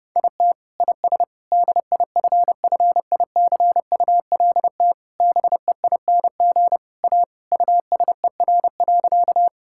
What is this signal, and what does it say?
Morse code, heard as a keyed tone, about 30 wpm, 700 Hz. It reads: IT IS DIFFICULT BEING A USER.
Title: Sentences from Top 1000 Words - Review Part 5 - 30wpm